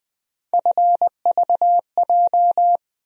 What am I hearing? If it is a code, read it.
Morse code, heard as a keyed tone, about 20 wpm, 700 Hz. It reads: FVJ
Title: USA Call Sign Postfixes - 3 Letters 20wpm